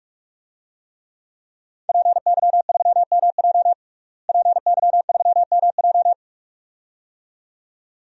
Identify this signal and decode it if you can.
Morse code, heard as a keyed tone, about 46 wpm, 700 Hz. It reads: PY3MJ PY3MJ
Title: DXCC-100 Call Signs - 5 Characters - Repeated 2x - 45wpm